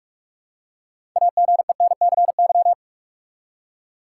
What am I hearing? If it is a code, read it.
Morse code, heard as a keyed tone, about 47 wpm, 700 Hz. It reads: AGENCY
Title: Six Letter Words - 45wpm